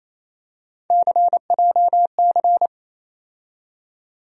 Morse code, heard as a keyed tone, about 28 wpm, 700 Hz. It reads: CJC
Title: USA Call Sign Postfixes - 3 Letters 28wpm